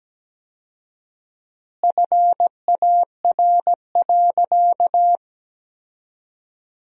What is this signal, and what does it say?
Morse code, heard as a keyed tone, about 17 wpm, 700 Hz. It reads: FAR.
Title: 4x Word-Spacing - Sentences from Top 400 Words - 17wpm